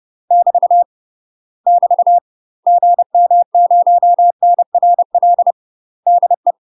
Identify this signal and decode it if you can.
Morse code, heard as a keyed tone, about 30 wpm, 700 Hz. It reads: <BT> <BT> GM0NRL DE
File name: UK MASTER QSOs TRAINING / 30WPM / UK MASTER QSOs 0015